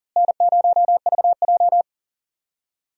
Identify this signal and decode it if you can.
Morse code, heard as a keyed tone, about 40 wpm, 700 Hz. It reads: N0VJ
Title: USA Call Signs - 4 Characters - Repeated 2x - 40wpm